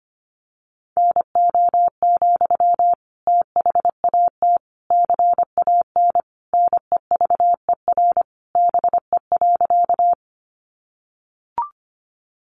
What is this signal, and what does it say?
Morse code, heard as a keyed tone, about 25 wpm, 700 Hz. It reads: NO, THAT CAN NEVER BE.
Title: Sentences from Top 300 Words 25wpm